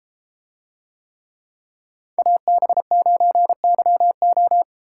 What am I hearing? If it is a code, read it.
Morse code, heard as a keyed tone, about 32 wpm, 700 Hz. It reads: AB9YO